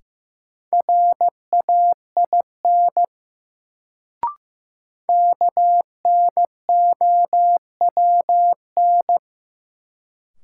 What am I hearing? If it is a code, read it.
Morse code, heard as a keyed tone, about 15 wpm, 700 Hz. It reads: RAIN KNOWN
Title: Sentences from Top 600 Words - New Word Drill - 15wpm